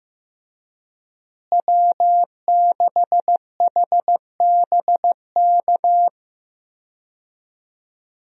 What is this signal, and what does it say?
Morse code, heard as a keyed tone, about 15 wpm, 700 Hz. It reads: W6HBK
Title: USA Call Signs - 5 Characters - Repeated 3x - 15wpm